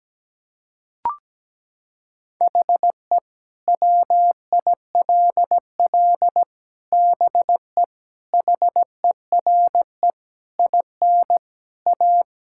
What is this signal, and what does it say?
Morse code, heard as a keyed tone, about 17 wpm, 700 Hz. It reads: HE WILL BE HERE IN A
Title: Sentences from Top 1000 Words - Review Part 5 - 17wpm